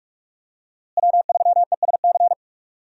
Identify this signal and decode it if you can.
Morse code, heard as a keyed tone, about 43 wpm, 700 Hz. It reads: W3ESC
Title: USA Call Signs - 5 Characters - Repeated 2x - 45wpm